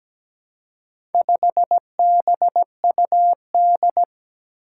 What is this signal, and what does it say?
Morse code, heard as a keyed tone, about 17 wpm, 700 Hz. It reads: 5BUD
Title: Four Challenging Letter-Number Combinations 17wpm